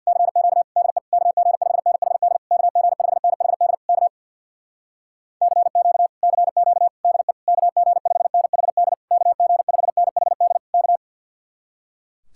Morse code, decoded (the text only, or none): CQ DE KG5NHD KG5NHD K CQ CQ DE KG5NHD KG5NHD K